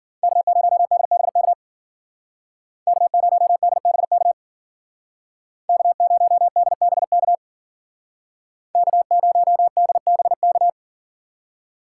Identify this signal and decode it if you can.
Morse code, heard as a keyed tone, about 48 wpm, 700 Hz. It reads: K0DBK K0DBK K0DBK K0DBK
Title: Speed Racing - USA 1x3 Call Signs 40wpm